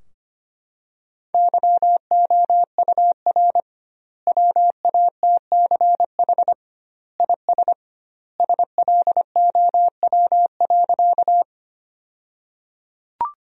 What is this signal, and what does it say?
Morse code, heard as a keyed tone, about 25 wpm, 700 Hz. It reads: YOUR WATCH IS SLOW.
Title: 2x Word-Spacing - Sentences from Top 400 Words - 25wpm